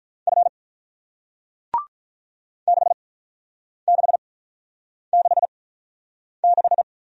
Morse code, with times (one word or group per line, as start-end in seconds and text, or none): 0.3-0.5 F
2.7-6.8 6666